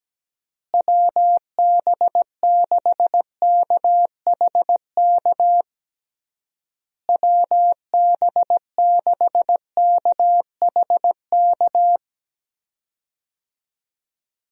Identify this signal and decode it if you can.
Morse code, heard as a keyed tone, about 17 wpm, 700 Hz. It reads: WB6KHK WB6KHK
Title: USA Call Signs - 6 Characters - Repeated 3x - 17wpm